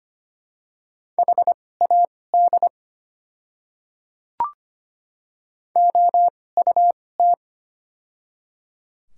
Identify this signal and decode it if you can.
Morse code, heard as a keyed tone, about 25 wpm, spaced as for 18 wpm, 700 Hz. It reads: HAD OUT
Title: 2x Character Spacing - Top 200 Words - 25wpm